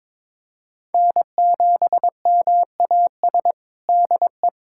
Morse code, heard as a keyed tone, about 22 wpm, 700 Hz. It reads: N7MAS DE
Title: USA Callsign DE Callsign - 5 Characters - 22wpm